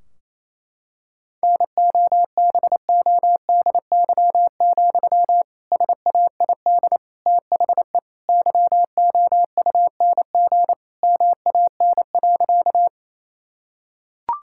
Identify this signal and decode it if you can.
Morse code, heard as a keyed tone, about 28 wpm, 700 Hz. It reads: NOBODY, SAID THE YOUNG MAN.
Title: Sentences from Top 1600 Words 28wpm